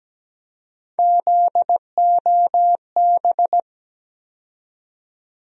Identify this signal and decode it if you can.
Morse code, heard as a keyed tone, about 17 wpm, 700 Hz. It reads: ZOB
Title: USA Call Sign Postfixes - 3 Letters 17wpm